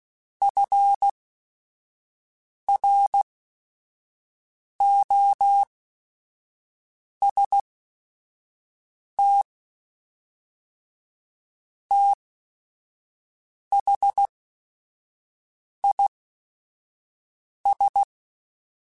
Morse code, heard as a keyed tone, about 16 wpm, spaced as for 5 wpm, 800 Hz. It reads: FROST THIS